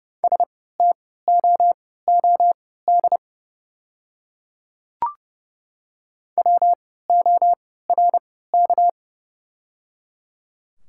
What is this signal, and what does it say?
Morse code, heard as a keyed tone, about 30 wpm, spaced as for 17 wpm, 700 Hz. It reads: STOOD WORK
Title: 3x Character Spacing - Top 500 Words - 30wpm